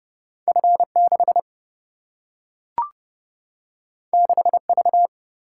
Morse code, heard as a keyed tone, about 29 wpm, 700 Hz. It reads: F 6 6 V
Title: Single-Letter Single-Number 30wpm